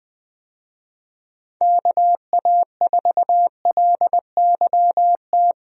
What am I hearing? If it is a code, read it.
Morse code, heard as a keyed tone, about 20 wpm, 700 Hz. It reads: KA4LYT
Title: USA Call Signs - 6 Characters - Repeated 3x - 20wpm